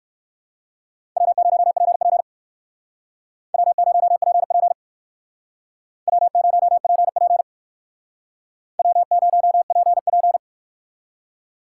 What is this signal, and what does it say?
Morse code, heard as a keyed tone, about 55 wpm, 700 Hz. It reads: W0PP W0PP W0PP W0PP
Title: Speed Racing - USA 1x2 Call Signs 45wpm